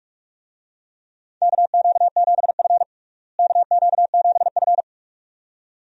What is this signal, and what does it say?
Morse code, heard as a keyed tone, about 46 wpm, 700 Hz. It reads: KQ7F KQ7F
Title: USA Call Signs - 4 Characters - Repeated 2x - 45wpm